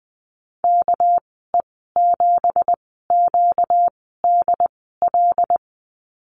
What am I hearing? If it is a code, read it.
Morse code, heard as a keyed tone, about 20 wpm, spaced as for 14 wpm, 700 Hz. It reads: KE7QDL
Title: 245 Lesson 39g - Letter z - Call Signs 20wpm